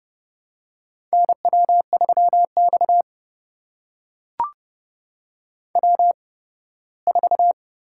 Morse code, heard as a keyed tone, about 28 wpm, spaced as for 18 wpm, 700 Hz. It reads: NW3X W4
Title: USA Call Signs - 4 Characters - 8x Character Spacing - 30wpm